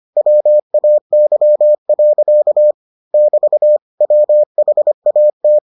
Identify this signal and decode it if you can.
Morse code, heard as a keyed tone, about 25 wpm, 600 Hz. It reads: WAY. <BT> WHAT